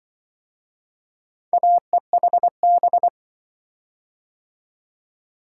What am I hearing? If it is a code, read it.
Morse code, heard as a keyed tone, about 25 wpm, 700 Hz. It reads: AEHB